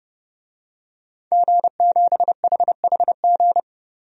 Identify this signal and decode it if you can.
Morse code, heard as a keyed tone, about 30 wpm, 700 Hz. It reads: G7HHG
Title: DXCC-100 Call Signs - 5 Characters - 30wpm